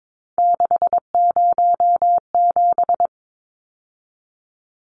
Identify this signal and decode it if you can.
Morse code, heard as a keyed tone, about 22 wpm, 700 Hz. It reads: 607
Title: Numbers - 3-Digit 22wpm